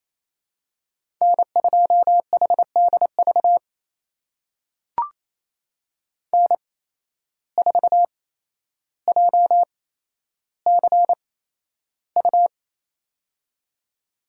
Morse code, heard as a keyed tone, about 28 wpm, spaced as for 12 wpm, 700 Hz. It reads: N2HDV N4JCU
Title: USA Call Signs - 5 Characters - 8x Character Spacing - 28wpm